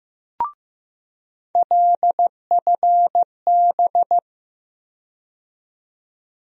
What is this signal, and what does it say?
Morse code, heard as a keyed tone, about 15 wpm, 700 Hz. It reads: LFB
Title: USA Call Sign Postfixes - 3 Letters 15wpm